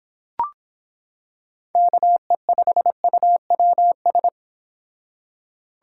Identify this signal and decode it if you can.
Morse code, heard as a keyed tone, about 25 wpm, 700 Hz. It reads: KE5UWS